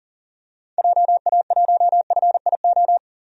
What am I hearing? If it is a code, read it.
Morse code, heard as a keyed tone, about 40 wpm, 700 Hz. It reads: JA1FIO
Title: DXCC-100 Call Signs - 6 Characters - 40wpm